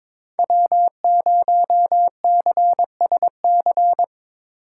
Morse code, heard as a keyed tone, about 24 wpm, 690 Hz. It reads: W0CSC
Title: Speed Racing - USA Call Signs 22wpm